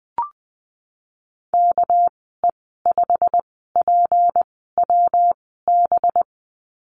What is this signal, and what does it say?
Morse code, heard as a keyed tone, about 20 wpm, 700 Hz. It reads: K E 5 P W B
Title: 200 Lesson 32g - Letter k - Call Signs 20wpm